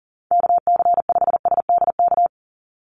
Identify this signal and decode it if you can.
Morse code, heard as a keyed tone, about 40 wpm, 700 Hz. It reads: KC5SDK
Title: USA Call Signs 40wpm